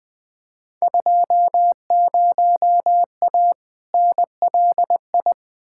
Morse code, heard as a keyed tone, about 20 wpm, 700 Hz. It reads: 20A NLI
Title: Field Day Exchanges Encore - 20wpm